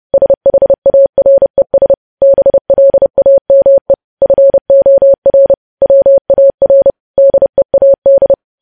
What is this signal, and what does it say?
Morse code, heard as a keyed tone, about 30 wpm, 550 Hz. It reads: SHARES BLAME FOR WAR DEAD
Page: news headlines in morse code at 30 wpm